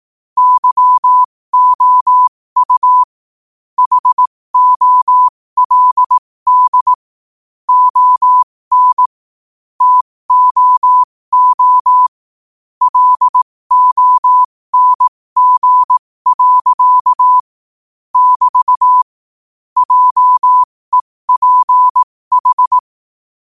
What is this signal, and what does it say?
Morse code, heard as a keyed tone, about 18 wpm, spaced as for 15 wpm, 1000 Hz. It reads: YOU HOLD ON TOO LONG. <BT> JEPH
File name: qotdfiles / 15wpm